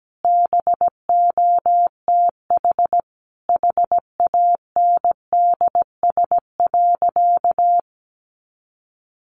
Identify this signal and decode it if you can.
Morse code, heard as a keyed tone, about 17 wpm, 700 Hz. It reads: BOTH HANDS.
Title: Easy Phrases 17wpm